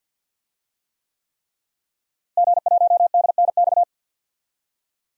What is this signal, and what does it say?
Morse code, heard as a keyed tone, about 52 wpm, 700 Hz. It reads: G1DNX
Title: DXCC-100 Call Signs - 5 Characters - Repeated 3x - 50wpm